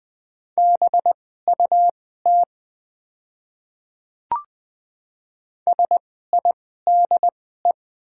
Morse code, heard as a keyed tone, about 20 wpm, spaced as for 15 wpm, 700 Hz. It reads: BUT SIDE
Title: 2x Character Spacing - Top 100 Words - 20wpm